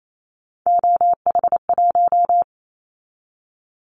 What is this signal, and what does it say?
Morse code, heard as a keyed tone, about 28 wpm, 700 Hz. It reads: OH1
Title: Callsign Prefixes for Top 50 DXCC - 28wpm